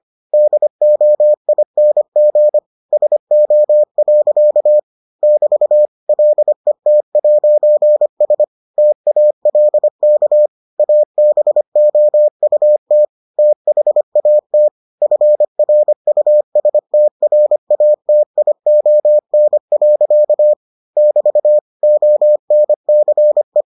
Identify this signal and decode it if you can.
Morse code, heard as a keyed tone, about 25 wpm, 600 Hz. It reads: DOING SO. <BT> LET'S TALK ABOUT THAT FRUSTRATION. <BT> ONCE